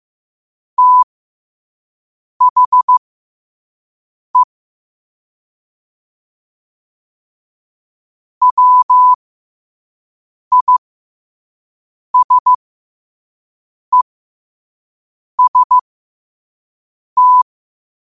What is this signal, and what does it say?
Morse code, heard as a keyed tone, about 15 wpm, spaced as for 5 wpm, 1000 Hz. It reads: THE WISEST
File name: qotdfiles / 5wpm